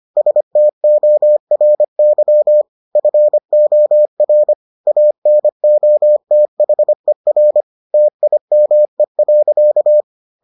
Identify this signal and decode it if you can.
Morse code, heard as a keyed tone, about 25 wpm, 600 Hz. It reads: STORY FOR ANOTHER TIME.